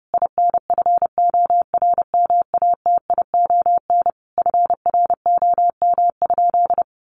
Morse code, heard as a keyed tone, about 30 wpm, 700 Hz. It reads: INFORMATION FROM?
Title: Sentences from Top 600 Words 30wpm